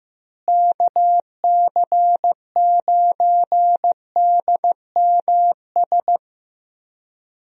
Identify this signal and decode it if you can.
Morse code, heard as a keyed tone, about 15 wpm, 700 Hz. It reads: KC9DMS
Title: USA Call Signs - 6 Characters - Repeated 3x - 15wpm